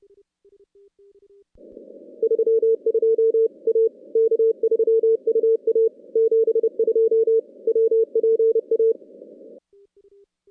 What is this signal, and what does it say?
Morse code, heard as a keyed tone, about 27 wpm, 410 Hz. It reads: 32 A K3UA 72 WPA
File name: contests / CW-SS-k5tr- -2100